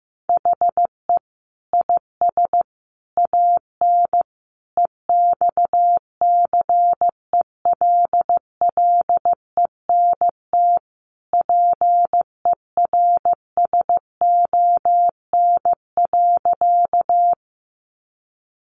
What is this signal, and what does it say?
Morse code, heard as a keyed tone, about 15 wpm, 700 Hz. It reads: HE IS AN EXCELLENT PERSON.
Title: Sentences from Top 1000 Words 15wpm